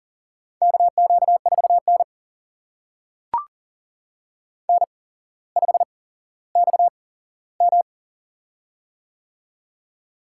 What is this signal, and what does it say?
Morse code, heard as a keyed tone, about 40 wpm, spaced as for 17 wpm, 700 Hz. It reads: KQ4N N5XM
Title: USA Call Signs - 4 Characters - 8x Character Spacing - 40wpm